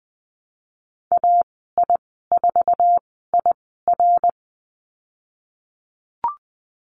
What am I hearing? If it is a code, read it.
Morse code, heard as a keyed tone, about 20 wpm, spaced as for 15 wpm, 700 Hz. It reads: AI4IR
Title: 122 Lesson 20g - Number 3 - Call Signs 20wpm